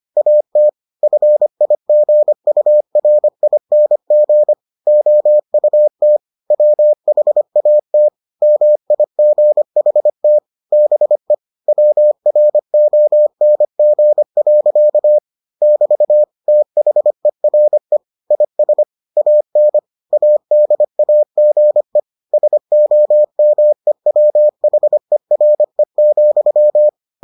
Morse code, heard as a keyed tone, about 25 wpm, 600 Hz. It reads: AT FIGURING OUT WHAT MIGHT BE WRONG. <BT> THERE IS AN ADAGE SOMEWHERE,